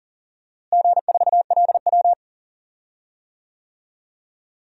Morse code, heard as a keyed tone, about 40 wpm, 700 Hz. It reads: G4LW
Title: DXCC-100 Call Signs - 4 Characters - Repeated 2x - 40wpm